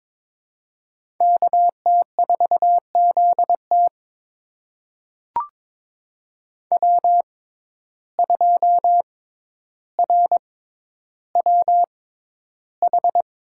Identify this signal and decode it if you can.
Morse code, heard as a keyed tone, about 22 wpm, spaced as for 11 wpm, 700 Hz. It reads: KT4ZT W2RWH
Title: USA Call Signs - 5 Characters - 6x Character Spacing - 22wpm